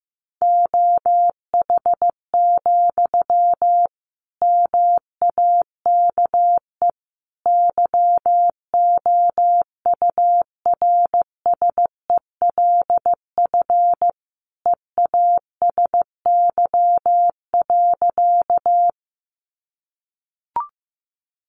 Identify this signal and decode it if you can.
Morse code, heard as a keyed tone, about 15 wpm, 700 Hz. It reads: OH, MAKE YOURSELF EASY.
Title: Sentences from Top 700 Words 15wpm